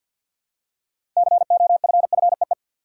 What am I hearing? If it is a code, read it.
Morse code, heard as a keyed tone, about 46 wpm, 700 Hz. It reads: COFFEE
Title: Six Letter Words - 50wpm